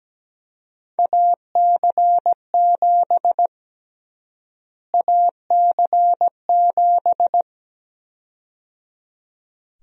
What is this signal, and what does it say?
Morse code, heard as a keyed tone, about 17 wpm, 700 Hz. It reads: AC7 AC7
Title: USA Call Sign Prefixes - Repeated 2x - 17wpm